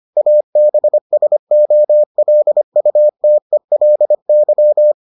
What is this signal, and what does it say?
Morse code, heard as a keyed tone, about 25 wpm, 600 Hz. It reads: ABSOLUTELY